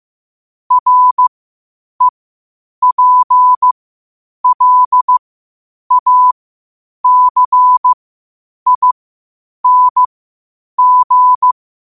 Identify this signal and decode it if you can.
Morse code, heard as a keyed tone, about 15 wpm, spaced as for 8 wpm, 1000 Hz. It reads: REPLACING